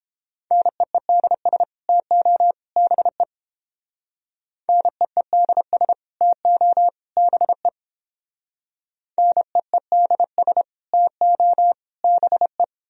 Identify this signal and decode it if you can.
Morse code, heard as a keyed tone, about 30 wpm, 700 Hz. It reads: NEEDS TO BE NEEDS TO BE NEEDS TO BE